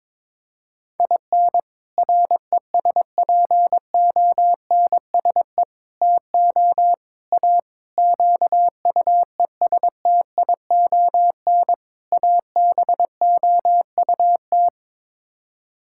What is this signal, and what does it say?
Morse code, heard as a keyed tone, about 22 wpm, 700 Hz. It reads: IN RESPONSE TO A QUESTION ABOUT
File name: practice / 22wpm / Sets of 6 Words - 22wpm